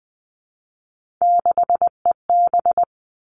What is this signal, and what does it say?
Morse code, heard as a keyed tone, about 20 wpm, 700 Hz. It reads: 6EB
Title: Three Challenging Letter-Number Combinations 20wpm